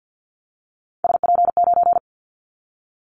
Morse code, heard as a keyed tone, about 50 wpm, 700 Hz. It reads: SP9
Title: Callsign Prefixes for Top 25 DXCC - Repeated 3x - 50wpm